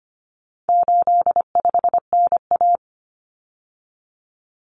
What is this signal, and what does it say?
Morse code, heard as a keyed tone, about 25 wpm, 700 Hz. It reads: 85NA
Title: Common Cut Numbers - 4-Digit 25wpm